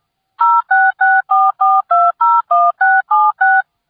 DTMF, heard as telephone touch tones, tones of about 220 ms. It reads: 06644201676